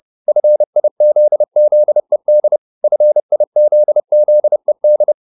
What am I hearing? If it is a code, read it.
Morse code, heard as a keyed tone, about 30 wpm, 600 Hz. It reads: FIZZED FIZZED